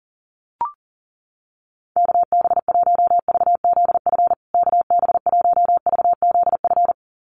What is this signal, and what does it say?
Morse code, heard as a keyed tone, about 40 wpm, 700 Hz. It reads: KB1VZF KB1VZF
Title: USA Call Signs Repeated 2x 40wpm